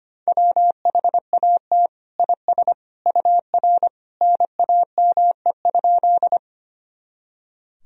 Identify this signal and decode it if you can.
Morse code, heard as a keyed tone, about 25 wpm, 700 Hz. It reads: WHAT IS UR NAME?
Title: Sentences from Top 200 Words and Abbreviations 25wpm